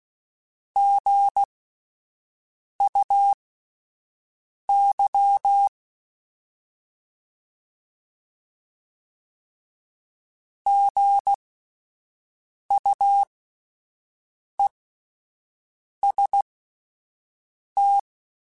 Morse code, heard as a keyed tone, about 16 wpm, spaced as for 5 wpm, 800 Hz. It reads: GUY GUEST
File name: Spaced Alphas / Spaced Alphas mp3s / Gs sp @10